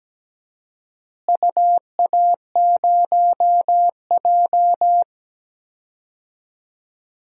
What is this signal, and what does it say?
Morse code, heard as a keyed tone, about 17 wpm, 700 Hz. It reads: UA0J
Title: DXCC-100 Call Signs - 4 Characters - Repeated 3x - 17wpm